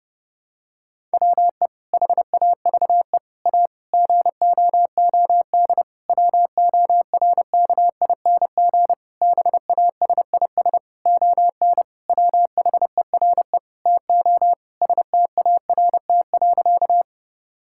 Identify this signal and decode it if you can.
Morse code, heard as a keyed tone, about 30 wpm, 700 Hz. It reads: WE HAVE A GOOD WORKING BASIS ON WHERE TO START.
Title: Sentences from Top 1000 Words - Review Part 5 - 30wpm